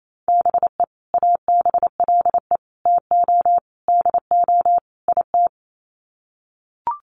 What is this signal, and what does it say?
Morse code, heard as a keyed tone, about 28 wpm, 700 Hz. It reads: BE ABLE TO DO IT
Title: Sets of 5 Words Encore2 28wpm